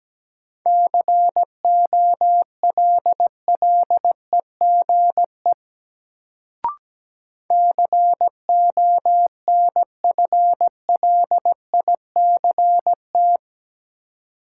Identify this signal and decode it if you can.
Morse code, heard as a keyed tone, about 17 wpm, 700 Hz. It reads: COLLEGE CONFLICT
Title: Sentences from Top 1300 Words - New Word Drill Alt - 17wpm